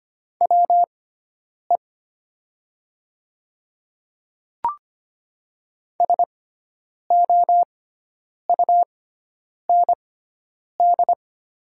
Morse code, heard as a keyed tone, about 25 wpm, spaced as for 9 wpm, 700 Hz. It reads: WE SOUND
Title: 6x Character Spacing - Top 100 Words - 25wpm